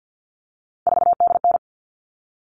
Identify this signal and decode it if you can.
Morse code, heard as a keyed tone, about 50 wpm, 700 Hz. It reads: 4DN
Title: All Cut Numbers - 3-Digit 50wpm